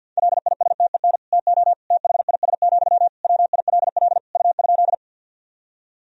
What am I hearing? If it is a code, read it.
Morse code, heard as a keyed tone, about 47 wpm, 700 Hz. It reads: LISTEN TO THIS, WILL U?
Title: Sentences from Top 400 Words and Abbreviations 50wpm